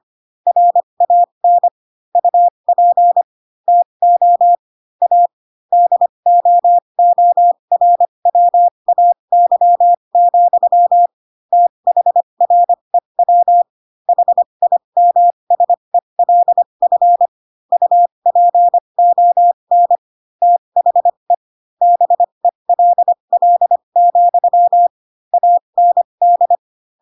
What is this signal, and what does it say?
Morse code, heard as a keyed tone, about 25 wpm, spaced as for 22 wpm, 700 Hz. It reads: RAN UP TO A DOORWAY, THREW HIMSELF UPON THE BELL, AND
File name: Pinocchio / Pinocchio ch6 18 25 ik1hge